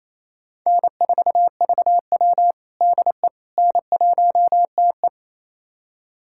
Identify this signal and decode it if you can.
Morse code, heard as a keyed tone, about 28 wpm, 700 Hz. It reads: N4VW DE N1TE
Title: USA Callsign DE Callsign - 4 Characters - 28wpm